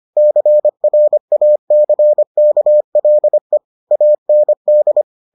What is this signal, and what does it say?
Morse code, heard as a keyed tone, about 25 wpm, 600 Hz. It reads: CRACKLE AND